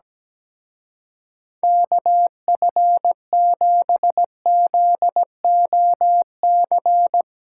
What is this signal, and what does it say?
Morse code, heard as a keyed tone, about 19 wpm, 700 Hz. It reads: KF7ZOC
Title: Speed Racing - USA 2x3 Call Signs 17wpm